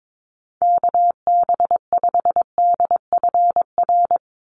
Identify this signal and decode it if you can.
Morse code, heard as a keyed tone, about 22 wpm, 700 Hz. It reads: KB5DFR
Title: USA Call Signs 22wpm